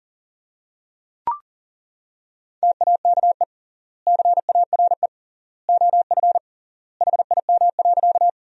Morse code, heard as a keyed tone, about 40 wpm, 700 Hz. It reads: TAKE CARE OF HIM.
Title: 3x Word-Spacing - Sentences from Top 300 Words - 40wpm